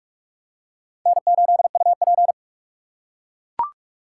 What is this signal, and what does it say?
Morse code, heard as a keyed tone, about 44 wpm, 700 Hz. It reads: N8UP